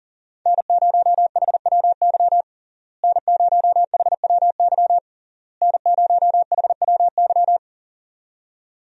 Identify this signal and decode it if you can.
Morse code, heard as a keyed tone, about 40 wpm, 700 Hz. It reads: N0HWY N0HWY N0HWY
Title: USA Call Signs - 5 Characters - Repeated 3x - 40wpm